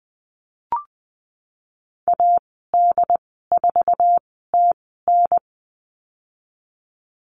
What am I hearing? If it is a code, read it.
Morse code, heard as a keyed tone, about 20 wpm, spaced as for 14 wpm, 700 Hz. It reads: AD4TN
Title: 069 Lesson 12g - Letter d - Call Signs 20wpm